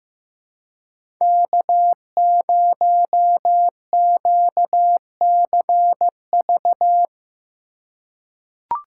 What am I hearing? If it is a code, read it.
Morse code, heard as a keyed tone, about 15 wpm, 700 Hz. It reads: K0QCV
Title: USA Call Signs - 2x Character Spacing - 15wpm